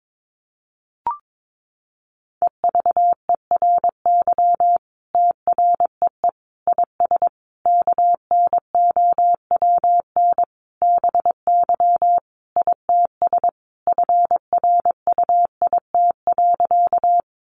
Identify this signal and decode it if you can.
Morse code, heard as a keyed tone, about 22 wpm, 700 Hz. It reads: EVERY TREE IS KNOWN BY ITS FRUIT.
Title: Sentences from Top 1300 Words 22wpm